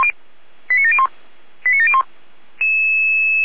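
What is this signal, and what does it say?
Five-tone selective call (ZVEI), heard as quick runs of five tones, tones of about 70 ms. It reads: 89701 89701
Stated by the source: ZVEI five-tone call received on SDR.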